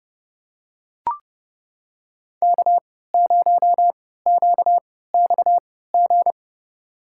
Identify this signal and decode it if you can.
Morse code, heard as a keyed tone, about 30 wpm, spaced as for 17 wpm, 690 Hz. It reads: K0QXG
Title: USA Call Signs - 3x Character Spacing - 30wpm